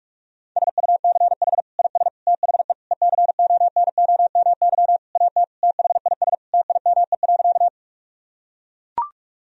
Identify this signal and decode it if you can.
Morse code, heard as a keyed tone, about 41 wpm, 700 Hz. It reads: SUCH IS THE ECONOMY AT THIS TIME.
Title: Sentences from Top 1000 Words - Review Part 3 - 45wpm